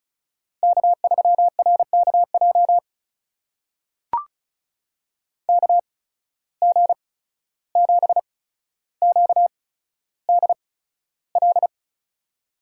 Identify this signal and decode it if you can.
Morse code, heard as a keyed tone, about 36 wpm, spaced as for 14 wpm, 700 Hz. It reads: K3RKJ KG7QDL
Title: USA Call Signs - 8x Character Spacing - 35wpm